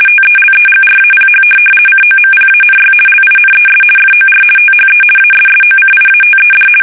RTTY, about 50 baud. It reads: TELEGRAPH COMPANY LINES FOR THE ASSOCIATED
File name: fsk850-50B / teletype-corp